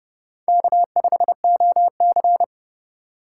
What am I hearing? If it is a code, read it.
Morse code, heard as a keyed tone, about 28 wpm, 700 Hz. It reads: K5OC